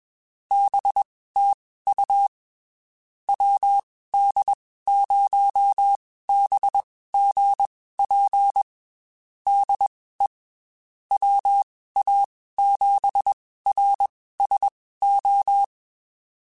Morse code, heard as a keyed tone, about 21 wpm, spaced as for 15 wpm, 800 Hz. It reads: BTU WD0BGP DE WA7RSO
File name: QSOs / QSOs-mp3s / WD0BGP 1@20